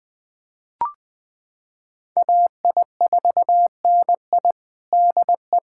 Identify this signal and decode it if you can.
Morse code, heard as a keyed tone, about 20 wpm, 700 Hz. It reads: AI4NI DE
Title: USA Callsign DE Callsign - 5 Characters - 20wpm